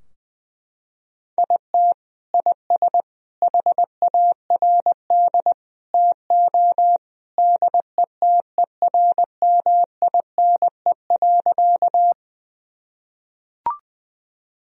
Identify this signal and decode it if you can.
Morse code, heard as a keyed tone, about 20 wpm, 700 Hz. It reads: IT IS HARD TO DETERMINE.
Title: Sentences from Top 1000 Words - Review Part 4 - 20wpm